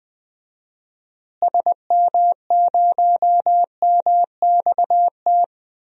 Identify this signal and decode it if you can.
Morse code, heard as a keyed tone, about 20 wpm, 700 Hz. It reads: SM0MXT
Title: DXCC-100 Call Signs - 6 Characters - Repeated 3x - 20wpm